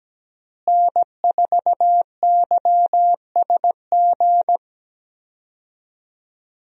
Morse code, heard as a keyed tone, about 17 wpm, 700 Hz. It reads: N4YSG
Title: USA Call Signs - 5 Characters - Repeated 3x - 17wpm